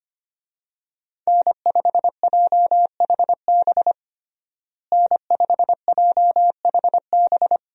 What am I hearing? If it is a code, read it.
Morse code, heard as a keyed tone, about 25 wpm, 700 Hz. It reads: N5JHB N5JHB
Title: USA Call Signs - 5 Characters - Repeated 3x - 25wpm